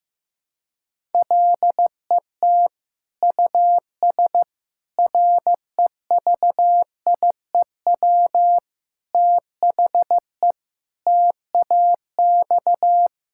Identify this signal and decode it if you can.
Morse code, heard as a keyed tone, about 15 wpm, 700 Hz. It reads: LET US REVIEW THE TAX
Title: Sentences from Top 1000 Words - Review Part 6 - 15wpm